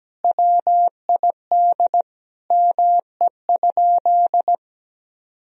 Morse code, heard as a keyed tone, about 17 wpm, 700 Hz. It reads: WID ME?
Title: Sentences from Top 300 Words and Abbreviations 17wpm